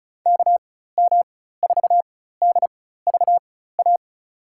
Morse code, heard as a keyed tone, about 36 wpm, spaced as for 16 wpm, 700 Hz. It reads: KM4DVA
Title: USA Call Signs - 6 Characters - 4x Character Spacing - 35wpm